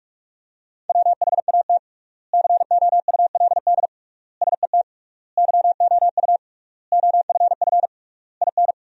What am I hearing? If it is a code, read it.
Morse code, heard as a keyed tone, about 46 wpm, 700 Hz. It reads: WHAT COULD SET YOU OFF IN
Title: 3x Word-Spacing - Sentences from Top 200 Words - 45wpm